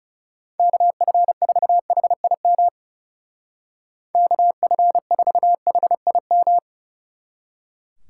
Morse code, heard as a keyed tone, about 35 wpm, 700 Hz. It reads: KF4HIM KF4HIM
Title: Speed Racing - USA Call Signs 30wpm